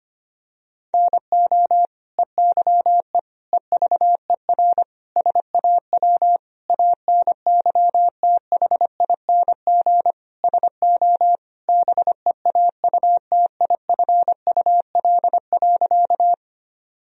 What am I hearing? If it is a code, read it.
Morse code, heard as a keyed tone, about 25 wpm, 700 Hz. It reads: NO EYE EVER SAW ANYTHING SO BEAUTIFUL.
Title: Sentences from Top 1000 Words - Review Part 1 - 25wpm